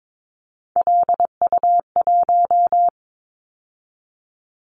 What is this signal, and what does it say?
Morse code, heard as a keyed tone, about 22 wpm, 700 Hz. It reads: LU1
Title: Callsign Prefixes for Top 100 DXCC - 22wpm